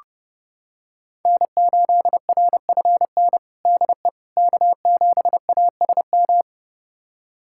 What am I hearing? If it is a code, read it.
Morse code, heard as a keyed tone, about 32 wpm, 700 Hz. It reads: N8RFN DE K7ASM
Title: USA Callsign DE Callsign - 5 Characters - 30wpm